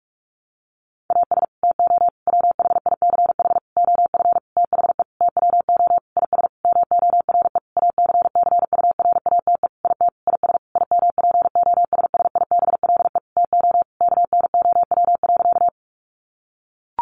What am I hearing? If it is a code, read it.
Morse code, heard as a keyed tone, about 44 wpm, 700 Hz. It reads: AS TO WHICH OF THE TWO IS MORE ACCURATE IT IS IMPOSSIBLE TO KNOW.
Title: Sentences from Top 1200 Words 45wpm